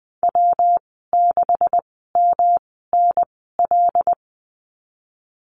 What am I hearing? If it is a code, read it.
Morse code, heard as a keyed tone, about 20 wpm, spaced as for 15 wpm, 700 Hz. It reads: W6MNL
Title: 128 Lesson 21g - Number 6 - Call Signs 20wpm